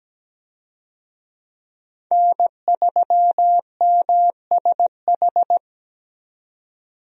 Morse code, heard as a keyed tone, about 17 wpm, 700 Hz. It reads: N3MSH
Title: USA Call Signs - 5 Characters - 17wpm